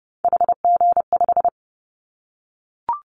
Morse code, heard as a keyed tone, about 30 wpm, 700 Hz. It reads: HG5
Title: Callsign Prefixes for Top 25 DXCC - 30wpm